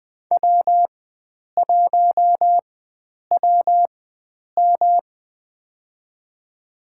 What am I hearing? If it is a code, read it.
Morse code, heard as a keyed tone, about 20 wpm, spaced as for 9 wpm, 700 Hz. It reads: W1WM
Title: USA Call Signs - 4 Characters - 4x Character Spacing - 20wpm